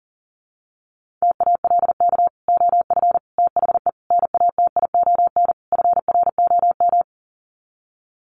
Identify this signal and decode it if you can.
Morse code, heard as a keyed tone, about 40 wpm, 700 Hz. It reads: TALK OF THE NATION FROM
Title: Sets of 5 Words 40wpm